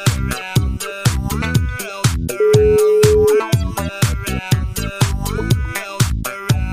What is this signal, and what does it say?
Ringback tone (Europe, 425 Hz), with noise.